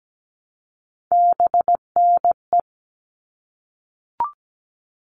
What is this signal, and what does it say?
Morse code, heard as a keyed tone, about 17 wpm, 700 Hz. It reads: BNE
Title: All Cut Numbers - 3-Digit 17wpm